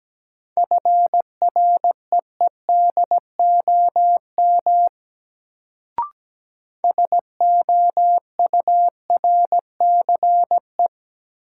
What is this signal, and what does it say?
Morse code, heard as a keyed tone, about 17 wpm, 700 Hz. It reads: FREEDOM SOURCE
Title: Sentences from Top 800 Words - New Word Drill - 17wpm